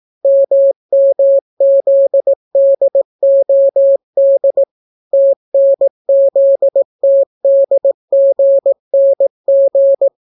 Morse code, heard as a keyed tone, about 18 wpm, 550 Hz. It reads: MMZDOD TNZTDGNG